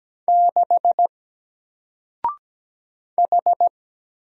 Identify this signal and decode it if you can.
Morse code, heard as a keyed tone, about 17 wpm, 700 Hz. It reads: 6 H
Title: Single Letter-Number Alt 17wpm